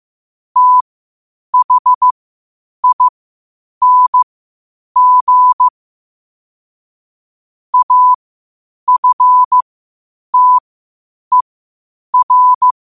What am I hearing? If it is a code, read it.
Morse code, heard as a keyed tone, about 15 wpm, spaced as for 8 wpm, 1000 Hz. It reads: THING AFTER
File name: qotdfiles / 8wpm